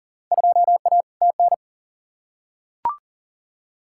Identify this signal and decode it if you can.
Morse code, heard as a keyed tone, about 40 wpm, 700 Hz. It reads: 2A TN
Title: Field Day Exchanges Encore - 40wpm